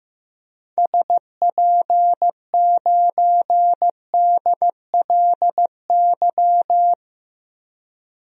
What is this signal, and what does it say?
Morse code, heard as a keyed tone, about 15 wpm, 700 Hz. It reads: SP9DLY
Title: DXCC-100 Call Signs - 6 Characters - Repeated 3x - 15wpm